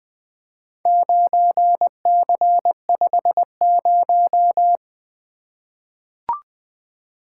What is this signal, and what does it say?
Morse code, heard as a keyed tone, about 20 wpm, 700 Hz. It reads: 9C50
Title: Four Letter-Number Combinations - 20wpm